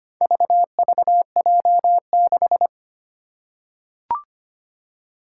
Morse code, heard as a keyed tone, about 25 wpm, 700 Hz. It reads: VVJ6
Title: Four Challenging Letter-Number Combinations 25wpm